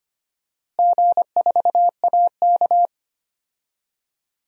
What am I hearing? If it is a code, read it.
Morse code, heard as a keyed tone, about 25 wpm, 700 Hz. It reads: G4AK
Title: DXCC-100 Call Signs - 4 Characters - Repeated 2x - 25wpm